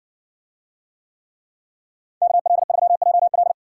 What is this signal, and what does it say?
Morse code, heard as a keyed tone, about 60 wpm, 700 Hz. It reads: KL3JL